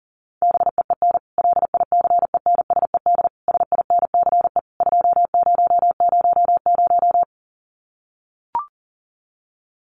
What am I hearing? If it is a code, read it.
Morse code, heard as a keyed tone, about 39 wpm, 700 Hz. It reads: BEEN LICENSED SINCE 2000
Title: Common QSO Phrases 40wpm